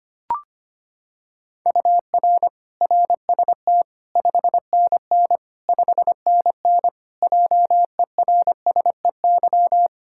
Morse code, heard as a keyed tone, about 25 wpm, 700 Hz. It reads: UR RST 5NN 5NN JERSEY
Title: Practice QSO - Exchange 1 - Beginning - 25wpm